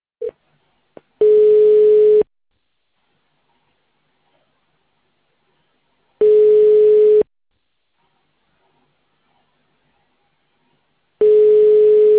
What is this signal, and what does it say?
Ringback tone (Europe, 425 Hz).